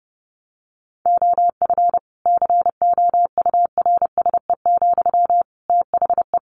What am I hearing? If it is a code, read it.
Morse code, heard as a keyed tone, about 30 wpm, 700 Hz. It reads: OF COURSE, THE
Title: Sentences from Top 1500 Words 30wpm